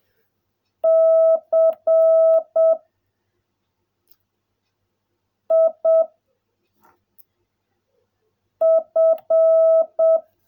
Morse code, heard as a keyed tone, about 7 wpm, 660 Hz. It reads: C I F
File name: jh8cbh / 2021cw / cw16a